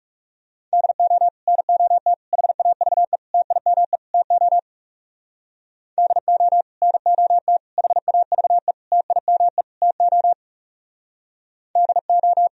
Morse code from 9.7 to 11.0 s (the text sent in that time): TO